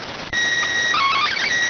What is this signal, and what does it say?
SSTV (slow-scan television), PD 290.